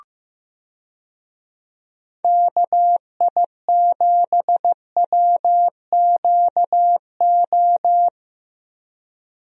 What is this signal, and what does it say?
Morse code, heard as a keyed tone, about 15 wpm, 700 Hz. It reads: KI7WQO
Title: USA Call Signs - 6 Characters - Repeated 3x - 15wpm